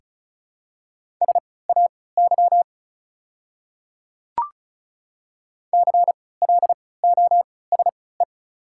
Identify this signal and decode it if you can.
Morse code, heard as a keyed tone, about 34 wpm, spaced as for 20 wpm, 700 Hz. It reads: SAY CLOSE
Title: 3x Character Spacing - Top 300 Words - 35wpm